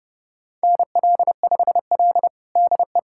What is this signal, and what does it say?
Morse code, heard as a keyed tone, about 30 wpm, 700 Hz. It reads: NL5L DE
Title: USA Callsign DE Callsign 30wpm